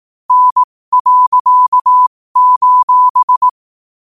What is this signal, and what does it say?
Morse code, heard as a keyed tone, about 18 wpm, spaced as for 15 wpm, 1000 Hz. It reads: N.: